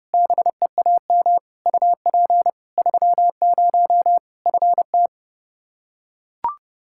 Morse code, heard as a keyed tone, about 30 wpm, 700 Hz. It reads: BEAM UP 30 FT
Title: K1USN SST QSO Practice 30wpm